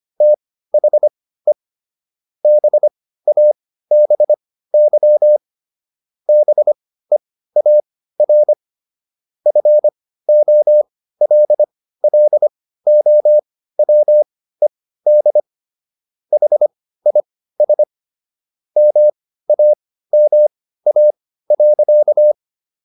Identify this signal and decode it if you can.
Morse code, heard as a keyed tone, about 25 wpm, spaced as for 15 wpm, 600 Hz. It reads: THE BABY BEAR FOLLOWED HIS MAMA.
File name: SS119 15